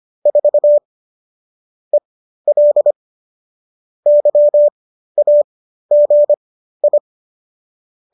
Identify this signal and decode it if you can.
Morse code, heard as a keyed tone, about 25 wpm, spaced as for 13 wpm, 590 Hz. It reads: 4 EL YAGI